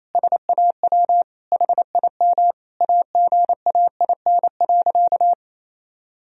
Morse code, heard as a keyed tone, about 28 wpm, 700 Hz. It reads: SAW HIM AGAIN.
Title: Sentences from Top 1000 Words - Review Part 1 - 28wpm